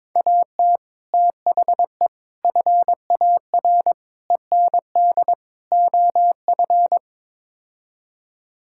Morse code, heard as a keyed tone, about 22 wpm, 700 Hz. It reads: AT THE FAR END OF
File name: practice / 22wpm / Build A Phrase - 5-Word Phrases - 22wpm